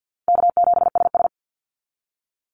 Morse code, heard as a keyed tone, about 50 wpm, 700 Hz. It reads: K7SS
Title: Call Signs Repeated 2x - 50wpm